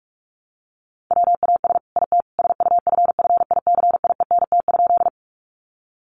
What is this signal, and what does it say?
Morse code, heard as a keyed tone, about 44 wpm, 700 Hz. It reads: WAS IT SUFFICIENT?
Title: Sentences from Top 1400 Words 45wpm